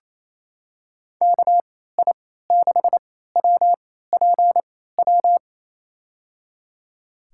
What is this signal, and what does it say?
Morse code, heard as a keyed tone, about 28 wpm, spaced as for 16 wpm, 700 Hz. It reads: KI6WPW
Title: USA Call Signs - 3x Character Spacing - 28wpm